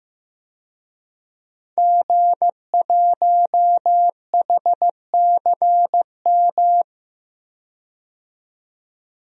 Morse code, heard as a keyed tone, about 15 wpm, 700 Hz. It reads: G1HCM
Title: DXCC-100 Call Signs - 5 Characters - 15wpm